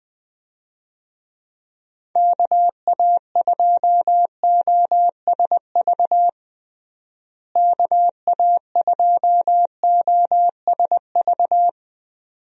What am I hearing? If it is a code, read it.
Morse code, heard as a keyed tone, about 20 wpm, 700 Hz. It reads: KA2OSV KA2OSV
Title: USA Call Signs - 6 Characters - Repeated 2x - 20wpm